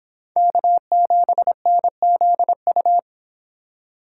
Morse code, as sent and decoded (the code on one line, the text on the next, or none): -.- --... -. --.. ..-
K7NZU